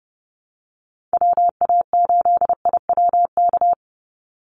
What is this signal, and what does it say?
Morse code, heard as a keyed tone, about 30 wpm, 700 Hz. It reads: WA8IWK
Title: Call Signs - 30wpm